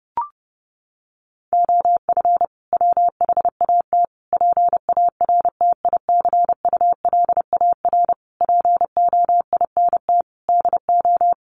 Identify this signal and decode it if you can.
Morse code, heard as a keyed tone, about 28 wpm, 700 Hz. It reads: OF WHAT PARTICULAR POINT DO
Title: Sentences from Top 1300 Words 30wpm